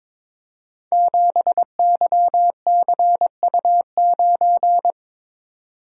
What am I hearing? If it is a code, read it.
Morse code, heard as a keyed tone, about 22 wpm, 700 Hz. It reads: 7YCU9
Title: Code Groups - 5 Letter:Numbers - 22wpm